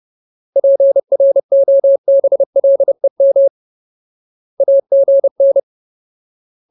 Morse code, heard as a keyed tone, about 30 wpm, 550 Hz. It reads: PROBLEM AGN
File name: Ultimate Word List / 30WPM Ultimate